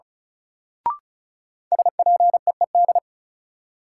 Morse code, heard as a keyed tone, about 36 wpm, 700 Hz. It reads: SPEED
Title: Sentences from Top 800 Words - New Word Drill - 35wpm